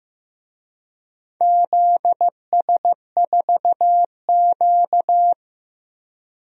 Morse code, heard as a keyed tone, about 15 wpm, 700 Hz. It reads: ZS4Q
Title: DXCC-100 Call Signs - 4 Characters - 15wpm